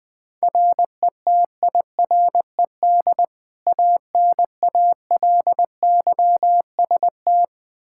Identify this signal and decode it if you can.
Morse code, heard as a keyed tone, about 20 wpm, 700 Hz. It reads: RETIRED ANALYST
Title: Practice QSO - Exchange 3 - More Info - 20wpm